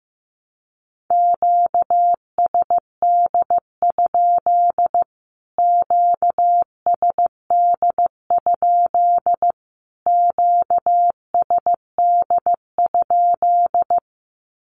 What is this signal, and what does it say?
Morse code, heard as a keyed tone, about 15 wpm, 700 Hz. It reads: QSD? QSD? QSD?
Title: Q-Codes Repeated 3x 15wpm